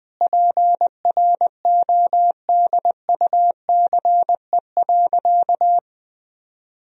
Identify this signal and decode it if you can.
Morse code, heard as a keyed tone, about 20 wpm, 700 Hz. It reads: PRODUCE.
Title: Sentences from Top 1000 Words - Review Part 3 - 20wpm